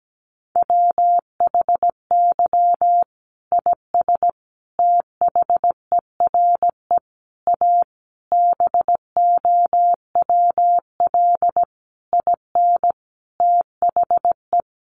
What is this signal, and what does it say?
Morse code, heard as a keyed tone, about 17 wpm, 700 Hz. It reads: WHY IS THERE A BOWL IN THE
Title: Sentences from Top 1200 Words 17wpm